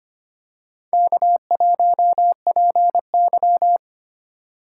Morse code, heard as a keyed tone, about 25 wpm, 700 Hz. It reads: K1PY
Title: USA Call Signs - 4 Characters - 2x Character Spacing - 25wpm